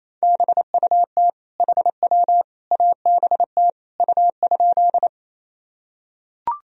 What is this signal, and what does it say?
Morse code, heard as a keyed tone, about 28 wpm, 700 Hz. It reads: BUT HW ABT U?
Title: Sentences from Top 100 Words and Abbreviations 28wpm